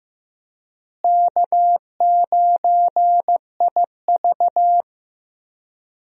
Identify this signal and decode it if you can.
Morse code, heard as a keyed tone, about 15 wpm, 700 Hz. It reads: K9IV
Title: USA Call Signs - 4 Characters - Repeated 3x - 15wpm